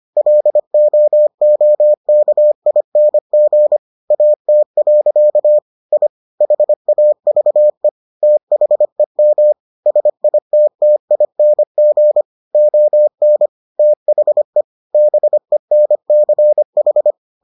Morse code, heard as a keyed tone, about 25 wpm, 600 Hz. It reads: LOOKING AT. I HAVE THEM SITTING ON THE BENCH